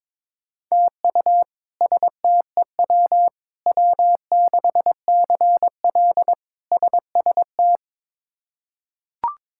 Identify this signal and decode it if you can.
Morse code, heard as a keyed tone, about 21 wpm, 700 Hz. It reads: TU STEW W6CL SST
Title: K1USN SST QSO Practice - 22wpm